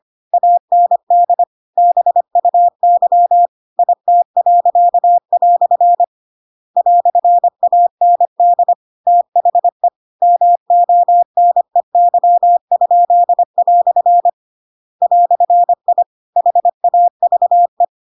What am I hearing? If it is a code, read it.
Morse code, heard as a keyed tone, about 25 wpm, 700 Hz. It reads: AND BUY IT." "AND THE MONEY?" "I HAVE
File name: Pinocchio / Pinocchio ch8 25 25 ik1hge